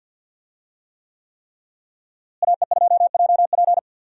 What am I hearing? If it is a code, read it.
Morse code, heard as a keyed tone, about 46 wpm, 700 Hz. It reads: AE2JP
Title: USA Call Signs - 5 Characters - Repeated 3x - 50wpm